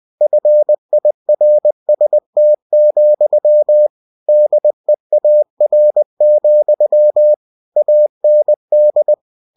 Morse code, heard as a keyed tone, about 20 wpm, 600 Hz. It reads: FIRST, DEAR, AND
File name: cw-books / winnie-the-pooh / winnie-20wpm-page-0107